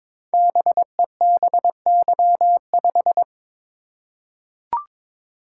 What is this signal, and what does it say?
Morse code, heard as a keyed tone, about 22 wpm, 700 Hz. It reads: BEBY5
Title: Code Groups - 5 Letter:Numbers - 22wpm